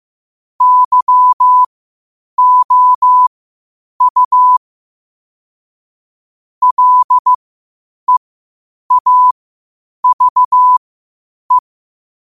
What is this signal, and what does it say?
Morse code, heard as a keyed tone, about 15 wpm, spaced as for 8 wpm, 1000 Hz. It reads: YOU LEAVE
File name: qotdfiles / 8wpm